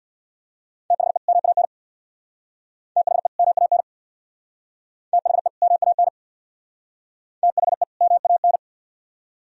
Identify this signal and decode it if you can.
Morse code, heard as a keyed tone, about 59 wpm, 700 Hz. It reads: THE MAN THE MAN THE MAN THE MAN
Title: Speed-Racing - Sets of 2 Words 50wpm